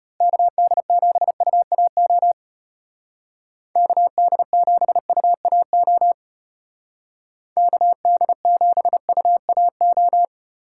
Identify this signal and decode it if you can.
Morse code, heard as a keyed tone, about 31 wpm, 700 Hz. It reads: KD7UAO KD7UAO KD7UAO